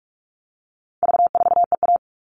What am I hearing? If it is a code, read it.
Morse code, heard as a keyed tone, about 44 wpm, 700 Hz. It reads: V4EA
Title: All Cut Numbers - 4-Digit 45wpm